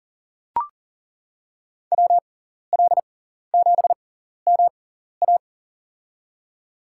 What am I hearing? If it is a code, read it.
Morse code, heard as a keyed tone, about 40 wpm, spaced as for 14 wpm, 700 Hz. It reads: WL7MA
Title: USA Call Signs - 5 Characters - 6x Character Spacing - 40wpm